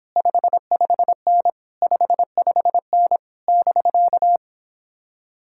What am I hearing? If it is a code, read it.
Morse code, heard as a keyed tone, about 26 wpm, 700 Hz. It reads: 55N 55N <BK>